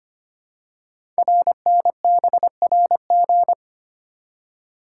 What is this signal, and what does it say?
Morse code, heard as a keyed tone, about 25 wpm, 700 Hz. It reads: RNBRG
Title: Code Groups - 5 Letters - 25wpm